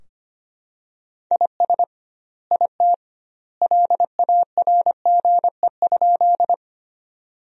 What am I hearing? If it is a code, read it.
Morse code, heard as a keyed tone, about 25 wpm, 700 Hz. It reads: IS IT LARGE?
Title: 2x Word-Spacing - Sentences from Top 200 Words - 25wpm